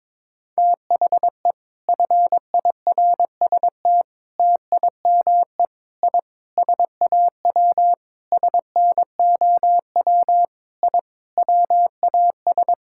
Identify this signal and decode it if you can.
Morse code, heard as a keyed tone, about 22 wpm, 700 Hz. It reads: THE FIRST TIME I SAW SNOW I WAS
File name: practice / 22wpm / Sentences with 4-Word Phrases - 22wpm